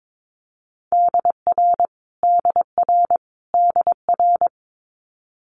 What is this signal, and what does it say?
Morse code, heard as a keyed tone, about 22 wpm, 700 Hz. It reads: DR DR DR
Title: Multi-letter Phonemes Repeated 3x 22wpm